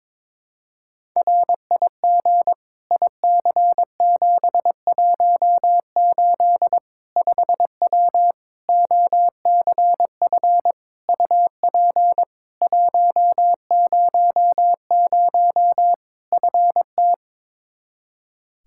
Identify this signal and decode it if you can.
Morse code, heard as a keyed tone, about 22 wpm, 700 Hz. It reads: RIG IC718 5W OCF UP 100 FT
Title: K1USN SST QSO Practice 22wpm